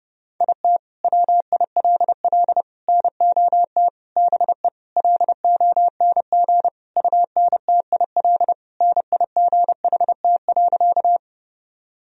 Morse code, heard as a keyed tone, about 30 wpm, 700 Hz. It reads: IT WILL NOT BE LONG UNTIL NIGHT.
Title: Sentences from Top 1000 Words - Review Part 1 - 30wpm